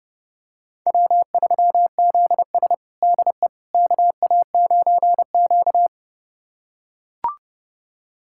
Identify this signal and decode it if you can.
Morse code, heard as a keyed tone, about 30 wpm, 700 Hz. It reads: W3ZS DE KA9Q
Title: USA Callsign DE Callsign - 4 Characters - 30wpm